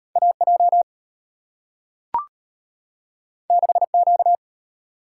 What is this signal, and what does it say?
Morse code, heard as a keyed tone, about 34 wpm, 690 Hz. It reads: AJ 6Q